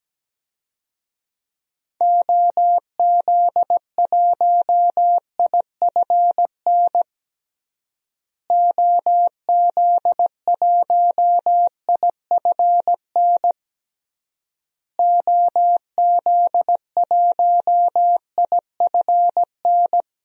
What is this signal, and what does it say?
Morse code, heard as a keyed tone, about 17 wpm, 700 Hz. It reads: OZ1IFN OZ1IFN OZ1IFN
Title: DXCC-100 Call Signs - 6 Characters - Repeated 3x - 17wpm